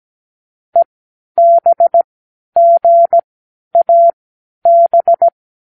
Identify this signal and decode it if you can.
Morse code, heard as a keyed tone, about 17 wpm, spaced as for 10 wpm, 680 Hz. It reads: EBGAB
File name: k7qo / Lesson12(A-H)